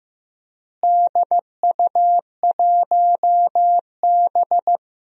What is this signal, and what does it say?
Morse code, heard as a keyed tone, about 15 wpm, 700 Hz. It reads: DU1B
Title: DXCC-100 Call Signs - 4 Characters - Repeated 3x - 15wpm